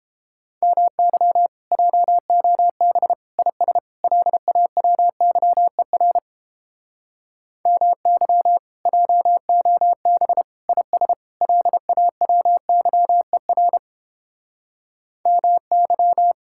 Morse code, heard as a keyed tone, about 29 wpm, 700 Hz. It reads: MY JOB IS LAWYER MY JOB IS LAWYER MY